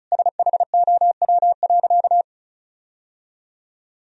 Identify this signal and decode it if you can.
Morse code, heard as a keyed tone, about 34 wpm, 700 Hz. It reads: SHOW.
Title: Sentences from Top 1000 Words - Review Part 2 - 35wpm